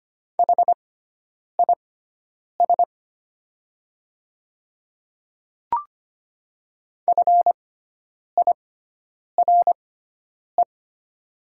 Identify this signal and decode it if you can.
Morse code, heard as a keyed tone, about 25 wpm, spaced as for 9 wpm, 690 Hz. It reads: HIS FIRE
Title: 6x Character Spacing - Top 500 Words - 25wpm